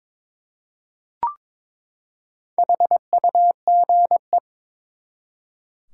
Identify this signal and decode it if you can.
Morse code, heard as a keyed tone, about 22 wpm, 700 Hz. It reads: HUGE
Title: Four Letter Words - 22wpm